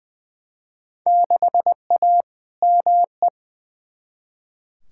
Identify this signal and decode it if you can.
Morse code, heard as a keyed tone, about 20 wpm, 700 Hz. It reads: 6A ME
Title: Field Day Exchanges - 20wpm